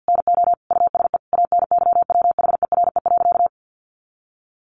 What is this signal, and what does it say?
Morse code, heard as a keyed tone, about 50 wpm, 700 Hz. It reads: NO USE ANYWHERE.
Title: Sentences from Top 1400 Words 50wpm